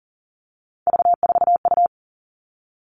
Morse code, heard as a keyed tone, about 40 wpm, 700 Hz. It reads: V4U
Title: All Cut Numbers - 3-Digit 40wpm